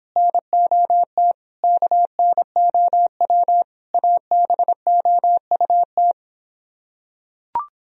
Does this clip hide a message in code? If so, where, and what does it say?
Yes: Morse code from 0.2 to 6.1 s, heard as a keyed tone, about 26 wpm, 700 Hz. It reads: NOT KNOW ABOUT